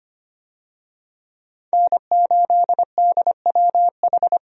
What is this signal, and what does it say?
Morse code, heard as a keyed tone, about 25 wpm, 700 Hz. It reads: N8DWH
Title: USA Call Signs - 5 Characters - 25wpm